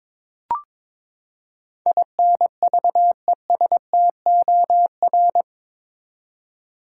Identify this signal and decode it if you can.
Morse code, heard as a keyed tone, about 22 wpm, 700 Hz. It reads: INVESTOR
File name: practice / 22wpm / Eight Letter Words - 22wpm